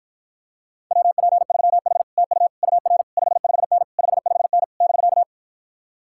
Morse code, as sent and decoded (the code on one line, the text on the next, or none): .-- .--. ...-- .... / - ..- / ..- .-. / ..... ..... -. / ..... ..... -. / -...-.-
WP3H TU UR 55N 55N <BK>